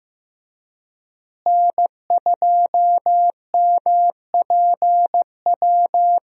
Morse code, heard as a keyed tone, about 15 wpm, 700 Hz. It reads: N2MPW
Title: USA Call Signs - 5 Characters - Repeated 3x - 15wpm